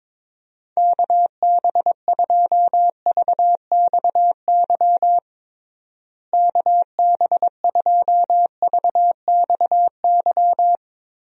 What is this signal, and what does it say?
Morse code, heard as a keyed tone, about 22 wpm, 700 Hz. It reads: KB2VXY KB2VXY
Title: USA Call Signs - 6 Characters - Repeated 3x - 22wpm